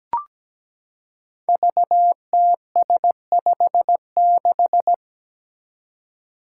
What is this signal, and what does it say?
Morse code, heard as a keyed tone, about 17 wpm, 700 Hz. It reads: VTS56
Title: Five Challenging Letter-Number Combinations 17wpm